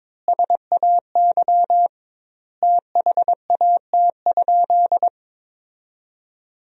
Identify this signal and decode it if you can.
Morse code, heard as a keyed tone, about 22 wpm, 700 Hz. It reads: SAY THAT?
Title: 2x Word-Spacing - Sentences from Top 200 Words - 22wpm